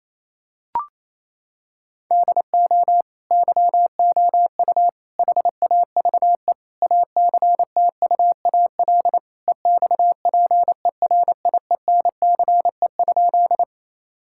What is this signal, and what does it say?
Morse code, heard as a keyed tone, about 28 wpm, 700 Hz. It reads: DO YOU HAVE ACTUAL EXPERIENCE?
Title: Sentences from Top 1000 Words - Review Part 5 - 28wpm